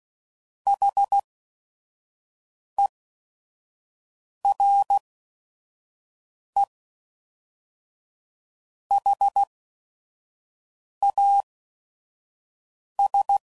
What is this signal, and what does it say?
Morse code, heard as a keyed tone, about 16 wpm, spaced as for 5 wpm, 800 Hz. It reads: HERE HAS
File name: CW Code Groups @5WPM / QSOs / QSO WD0FBT